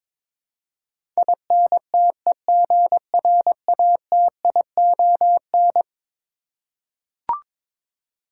Morse code, heard as a keyed tone, about 22 wpm, 700 Hz. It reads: INTEGRATION
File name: practice / 22wpm / Eleven Letter Words - 22wpm